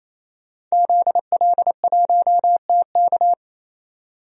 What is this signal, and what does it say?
Morse code, heard as a keyed tone, about 27 wpm, 690 Hz. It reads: ZL1TK
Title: DXCC-100 Call Signs - 5 Characters - 28wpm